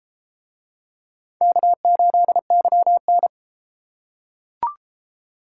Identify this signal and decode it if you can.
Morse code, heard as a keyed tone, about 33 wpm, 700 Hz. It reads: K8YN